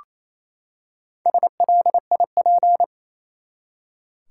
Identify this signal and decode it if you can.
Morse code, heard as a keyed tone, about 28 wpm, 700 Hz. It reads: SLIP
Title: Four Letter Words - 28wpm